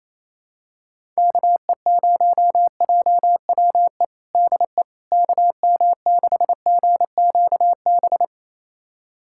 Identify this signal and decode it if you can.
Morse code, heard as a keyed tone, about 28 wpm, 700 Hz. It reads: KE0JWE DE KM6GQB
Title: USA Callsign DE Callsign 28wpm